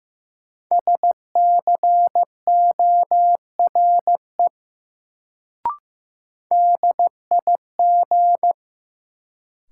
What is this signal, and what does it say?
Morse code, heard as a keyed tone, about 15 wpm, 700 Hz. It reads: SCORE DIG
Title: Sentences from Top 1600 Words - New Word Drill Alt - 15wpm